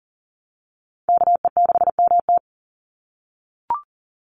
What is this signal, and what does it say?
Morse code, heard as a keyed tone, about 40 wpm, 700 Hz. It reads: KE6MT
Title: NA SOTA Chaser Callsigns 40wpm